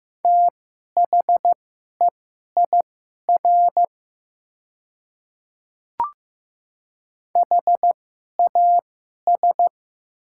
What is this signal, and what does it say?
Morse code, heard as a keyed tone, about 15 wpm, spaced as for 11 wpm, 690 Hz. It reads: THEIR HAS
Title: 2x Character Spacing - Top 200 Words - 15wpm